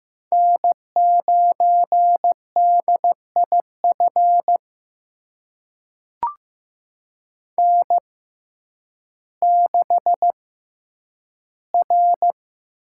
Morse code, heard as a keyed tone, about 15 wpm, spaced as for 9 wpm, 700 Hz. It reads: N9DIF N6R
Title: USA Call Signs - 6x Character Spacing - 15wpm